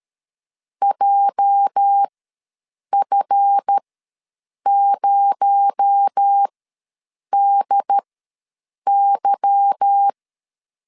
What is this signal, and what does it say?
Morse code, heard as a keyed tone, about 13 wpm, spaced as for 7 wpm, 800 Hz. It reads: JF0DY
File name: LEZIONI / 8 YJ45 / 8 YJ ws 12wpm 3342